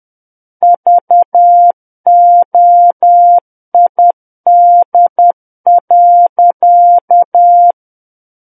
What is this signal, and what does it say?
Morse code, heard as a keyed tone, about 10 wpm, 700 Hz. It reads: VOID.
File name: Book One / 01-Chapter 1 - 10 WPM